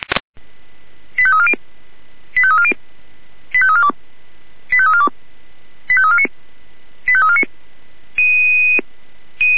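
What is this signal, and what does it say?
Five-tone selective call (ZVEI), heard as quick runs of five tones, tones of about 70 ms. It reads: 85340 85340 85342 85342 85349 85349
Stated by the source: ZVEI five-tone selective call received on an SDR.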